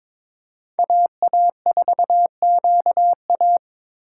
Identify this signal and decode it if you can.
Morse code, heard as a keyed tone, about 22 wpm, 700 Hz. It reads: AA4QA
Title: USA Call Signs - 2x Character Spacing - 22wpm